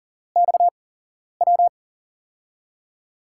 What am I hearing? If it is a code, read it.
Morse code, heard as a keyed tone, about 40 wpm, spaced as for 11 wpm, 700 Hz. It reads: XW
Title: USA Call Signs - 6 Characters - 8x Character Spacing - 40wpm